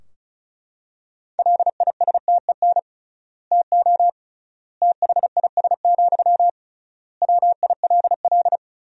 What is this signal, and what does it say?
Morse code, heard as a keyed tone, about 34 wpm, 700 Hz. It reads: LISTEN TO THIS, WILL
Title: 3x Word-Spacing - Sentences from Top 400 Words - 35wpm